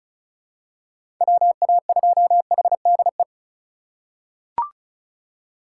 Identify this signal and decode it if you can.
Morse code, heard as a keyed tone, about 36 wpm, 690 Hz. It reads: WA2HDE
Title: USA Call Signs - 6 Characters - 4x Character Spacing - 35wpm